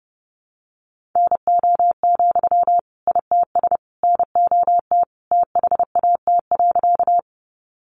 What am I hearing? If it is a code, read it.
Morse code, heard as a keyed tone, about 30 wpm, 700 Hz. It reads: NO, ITS NOT THAT.
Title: Sentences from Top 600 Words 30wpm